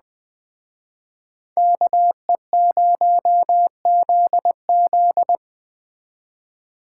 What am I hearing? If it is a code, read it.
Morse code, heard as a keyed tone, about 20 wpm, 700 Hz. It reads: KE0ZZ
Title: Speed Racing - USA 2x2 Call Signs 17wpm